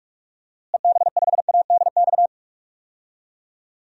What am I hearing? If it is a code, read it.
Morse code, heard as a keyed tone, about 41 wpm, 690 Hz. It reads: EB5ADX
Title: DXCC-100 Call Signs - 6 Characters - Repeated 2x - 45wpm